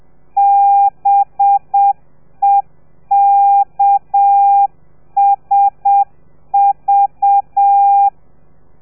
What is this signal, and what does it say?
Morse code, heard as a keyed tone, about 7 wpm, 800 Hz. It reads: BEKSV
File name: telegrafia / i2tfj / Lez4 30